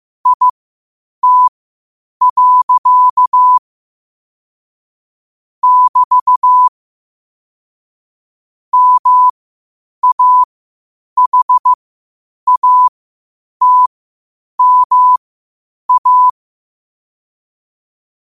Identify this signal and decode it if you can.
Morse code, heard as a keyed tone, about 15 wpm, spaced as for 8 wpm, 1000 Hz. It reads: IT. <BT> MAHATMA
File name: qotdfiles / 8wpm